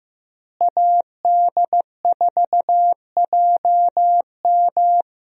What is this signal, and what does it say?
Morse code, heard as a keyed tone, about 15 wpm, 700 Hz. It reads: AD4JM
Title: USA Call Signs - 5 Characters - Repeated 3x - 15wpm